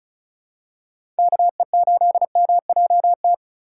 Morse code, heard as a keyed tone, about 36 wpm, 700 Hz. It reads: KE8MJT
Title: USA Call Signs - 6 Characters - 2x Character Spacing - 35wpm